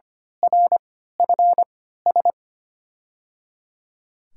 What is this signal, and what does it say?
Morse code, heard as a keyed tone, about 25 wpm, spaced as for 14 wpm, 700 Hz. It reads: RFS
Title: USA Call Signs - 3x Character Spacing - 25wpm